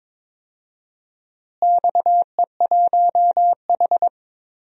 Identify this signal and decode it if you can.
Morse code, heard as a keyed tone, about 22 wpm, 700 Hz. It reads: XE1H
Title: DXCC-100 Call Signs - 4 Characters - Repeated 3x - 22wpm